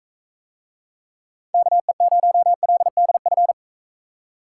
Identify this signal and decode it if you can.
Morse code, heard as a keyed tone, about 45 wpm, 700 Hz. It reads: KE0LDF